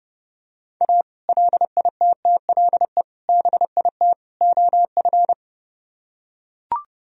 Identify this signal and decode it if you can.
Morse code, heard as a keyed tone, about 31 wpm, 700 Hz. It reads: A LITTLE BIT OF